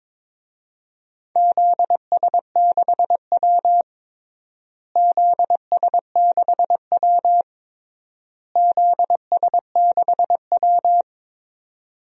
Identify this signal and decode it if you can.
Morse code, heard as a keyed tone, about 22 wpm, 700 Hz. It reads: ZS6W ZS6W ZS6W
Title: DXCC-100 Call Signs - 4 Characters - Repeated 3x - 22wpm